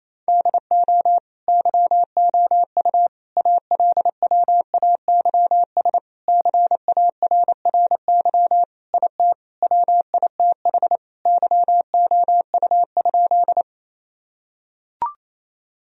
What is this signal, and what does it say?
Morse code, heard as a keyed tone, about 28 wpm, 700 Hz. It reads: DO YOU ALWAYS CARRY IT WITH YOU?
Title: Sentences from Top 1000 Words - Review Part 1 - 28wpm